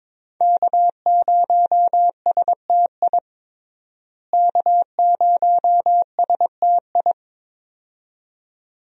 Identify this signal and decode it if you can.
Morse code, heard as a keyed tone, about 22 wpm, 700 Hz. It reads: K0STI K0STI
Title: USA Call Signs - 5 Characters - Repeated 2x - 22wpm